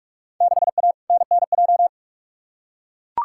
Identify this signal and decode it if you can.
Morse code, heard as a keyed tone, about 43 wpm, 700 Hz. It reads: 6A NNJ
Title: Field Day Exchanges Encore - 45wpm